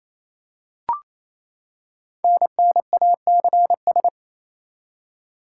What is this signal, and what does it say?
Morse code, heard as a keyed tone, about 27 wpm, 700 Hz. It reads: NNACS
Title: Code Groups - 5 Letters - 28wpm